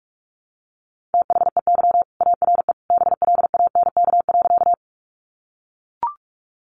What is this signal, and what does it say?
Morse code, heard as a keyed tone, about 44 wpm, 700 Hz. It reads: THEY ARE BLANK.
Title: Sentences from Top 1500 Words 45wpm